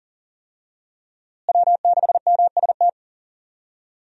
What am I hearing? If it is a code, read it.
Morse code, heard as a keyed tone, about 40 wpm, 700 Hz. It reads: W6MST
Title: USA Call Signs - 5 Characters - Repeated 3x - 40wpm